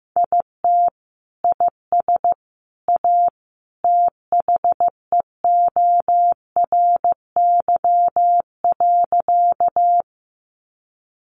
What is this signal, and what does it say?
Morse code, heard as a keyed tone, about 15 wpm, 700 Hz. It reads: IT IS A THEORY.
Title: Sentences from Top 600 Words 15wpm